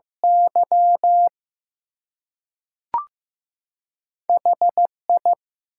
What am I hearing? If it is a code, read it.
Morse code, heard as a keyed tone, about 15 wpm, 700 Hz. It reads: Y HI
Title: Two Letter Words - 15wpm